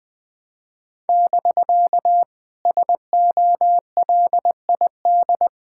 Morse code, heard as a keyed tone, about 20 wpm, 700 Hz. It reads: <BK> SOLID
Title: Speed-Racing Common QSO Phrases - 17wpm